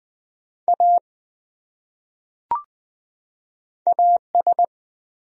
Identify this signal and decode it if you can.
Morse code, heard as a keyed tone, about 20 wpm, 700 Hz. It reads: A AS
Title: Top 100 Words 20wpm